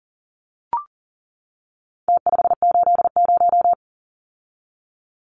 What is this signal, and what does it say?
Morse code, heard as a keyed tone, about 40 wpm, 700 Hz. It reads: T580
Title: Numbers - Contest Style - 4-Digit 40wpm